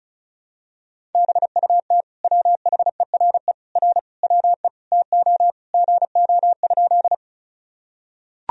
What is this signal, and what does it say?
Morse code, heard as a keyed tone, about 36 wpm, 700 Hz. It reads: BUT WHERE R WE TO GO?
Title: Sentences from Top 200 Words and Abbreviations 35wpm